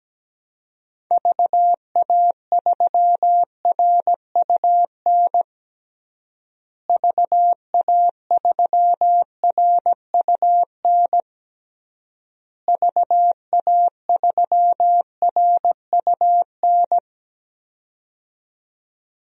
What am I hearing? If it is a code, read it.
Morse code, heard as a keyed tone, about 17 wpm, 700 Hz. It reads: VA3RUN VA3RUN VA3RUN
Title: DXCC-100 Call Signs - 6 Characters - Repeated 3x - 17wpm